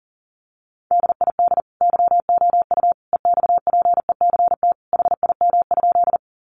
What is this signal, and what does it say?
Morse code, heard as a keyed tone, about 40 wpm, 700 Hz. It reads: DID YOU EXPECT HIM?
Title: Sentences from Top 1100 Words 40wpm